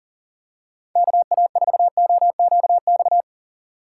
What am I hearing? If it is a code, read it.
Morse code, heard as a keyed tone, about 40 wpm, 700 Hz. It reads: KA4OQX
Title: USA Call Signs - 6 Characters - 2x Character Spacing - 40wpm